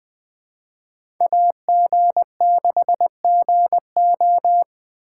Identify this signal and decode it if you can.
Morse code, heard as a keyed tone, about 20 wpm, 700 Hz. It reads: AG6GO
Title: Speed Racing - USA Call Signs 17wpm